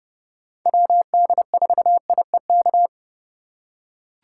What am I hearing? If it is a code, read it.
Morse code, heard as a keyed tone, about 30 wpm, 700 Hz. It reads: WD4IEK
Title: USA Call Signs - 6 Characters - 3x Character Spacing - 30wpm